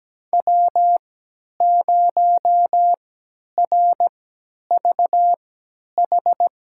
Morse code, heard as a keyed tone, about 17 wpm, spaced as for 10 wpm, 700 Hz. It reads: W0RVH
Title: USA Call Signs - 5 Characters - 3x Character Spacing - 17wpm